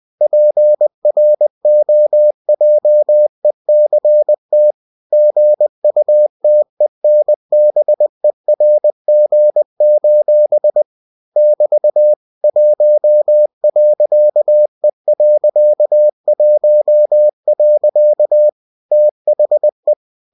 Morse code, heard as a keyed tone, about 20 wpm, 600 Hz. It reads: PROJECT GUTENBERG: <BT> 1.E.1. THE
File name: cw-books / winnie-the-pooh / winnie-20wpm-page-0114